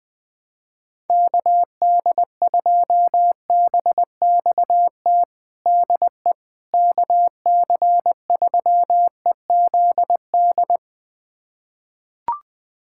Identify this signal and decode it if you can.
Morse code, heard as a keyed tone, about 21 wpm, 700 Hz. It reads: KD2BXT DE KC3EZD
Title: USA Callsign DE Callsign 20wpm